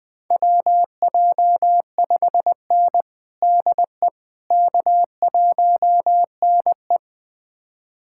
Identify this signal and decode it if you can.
Morse code, heard as a keyed tone, about 20 wpm, 700 Hz. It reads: WJ5N DE K1NE
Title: USA Callsign DE Callsign - 4 Characters - 20wpm